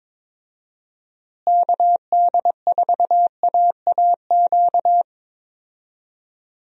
Morse code, heard as a keyed tone, about 22 wpm, 700 Hz. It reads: KD4AAQ
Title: USA Call Signs - 6 Characters - Repeated 3x - 22wpm